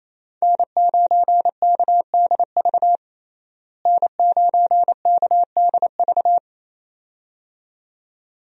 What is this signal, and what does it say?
Morse code, heard as a keyed tone, about 27 wpm, 700 Hz. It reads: N9KDV N9KDV
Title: USA Call Signs - 5 Characters - Repeated 2x - 28wpm